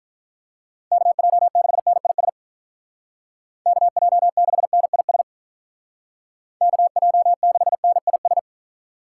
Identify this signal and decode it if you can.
Morse code, heard as a keyed tone, about 49 wpm, 700 Hz. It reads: KJ6NIS KJ6NIS KJ6NIS